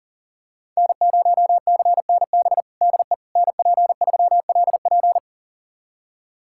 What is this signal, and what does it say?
Morse code, heard as a keyed tone, about 40 wpm, 700 Hz. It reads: N0CNB DE NP3LP
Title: USA Callsign DE Callsign - 5 Characters - 40wpm